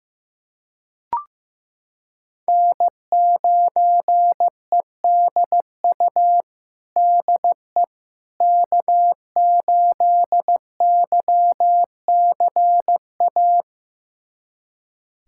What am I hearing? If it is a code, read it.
Morse code, heard as a keyed tone, about 15 wpm, 700 Hz. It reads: N9EDU DE K8YCA
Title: USA Callsign DE Callsign - 5 Characters - 15wpm